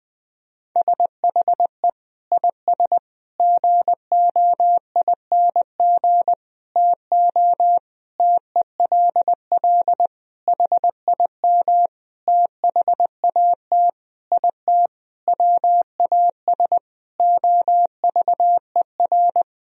Morse code, heard as a keyed tone, about 21 wpm, 700 Hz. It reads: SHE IS GOING TO TELL HIM THAT IT WAS OVER
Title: Sentences from Top 1000 Words - Review Part 1 - 20wpm